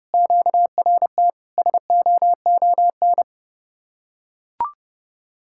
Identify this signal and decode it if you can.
Morse code, heard as a keyed tone, about 32 wpm, 700 Hz. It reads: QRT SOON